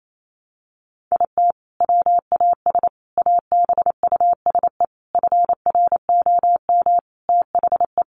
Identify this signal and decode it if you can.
Morse code, heard as a keyed tone, about 28 wpm, 700 Hz. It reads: IT WAS ABUSE FROM THE
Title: Sentences from Top 1300 Words 28wpm